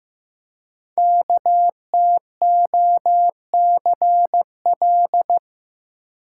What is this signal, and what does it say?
Morse code, heard as a keyed tone, about 15 wpm, 700 Hz. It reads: KTOCL
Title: Code Groups - 5 Letters - 15wpm